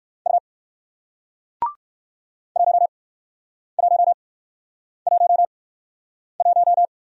Morse code, heard as a keyed tone, about 54 wpm, spaced as for 9 wpm, 700 Hz. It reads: U 1111
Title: Speed-Racing Single Letter-Number 45wpm